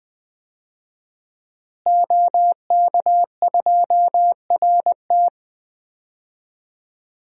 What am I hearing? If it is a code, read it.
Morse code, heard as a keyed tone, about 20 wpm, 700 Hz. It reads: OK2RT
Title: DXCC-100 Call Signs - 5 Characters - Repeated 3x - 20wpm